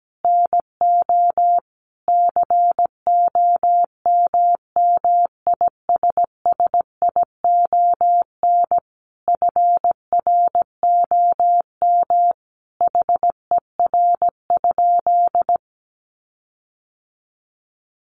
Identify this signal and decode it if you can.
Morse code, heard as a keyed tone, about 17 wpm, 700 Hz. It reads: NO COMMISSION FROM HER?
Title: Sentences from Top 1400 Words 17wpm